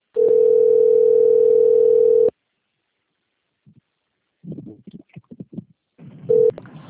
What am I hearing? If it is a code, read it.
Ringback tone (North America), 440 + 480 Hz.